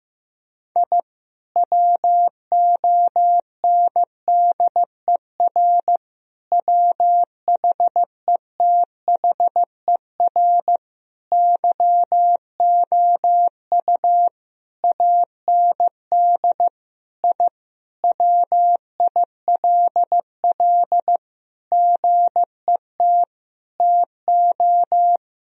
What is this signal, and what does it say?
Morse code, heard as a keyed tone, about 15 wpm, 700 Hz. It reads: I WONDER WHETHER YOU AND I WILL GET TO
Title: Sentences from Top 1000 Words - Review Part 3 - 15wpm